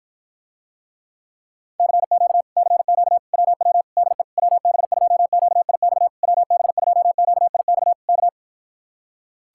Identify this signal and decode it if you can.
Morse code, heard as a keyed tone, about 54 wpm, 700 Hz. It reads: CQ CQ WW DE WB2QIX WB2QIX K